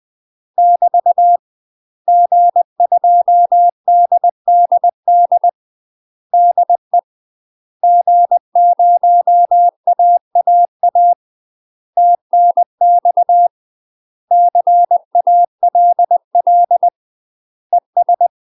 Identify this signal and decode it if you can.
Morse code, heard as a keyed tone, about 20 wpm, 700 Hz. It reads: <BT> G2DDD DE G0AAA TNX CALL ES
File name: UK MASTER QSOs TRAINING/20WPM